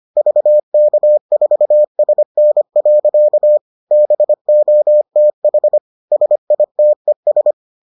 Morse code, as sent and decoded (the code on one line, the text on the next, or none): ...- -.- ....- ... -. .-.-.- / -... --- - .... / ... .. - . ...
VK4SN. BOTH SITES